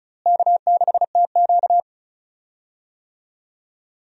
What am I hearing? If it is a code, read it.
Morse code, heard as a keyed tone, about 36 wpm, 700 Hz. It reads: K6TQ
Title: USA Call Signs - 4 Characters - Repeated 3x - 35wpm